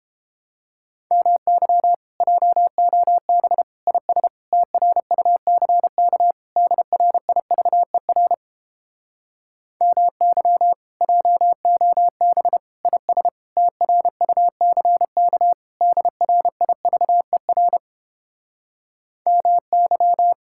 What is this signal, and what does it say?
Morse code, heard as a keyed tone, about 29 wpm, 700 Hz. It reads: MY JOB IS TRUCK DRIVER MY JOB IS TRUCK DRIVER MY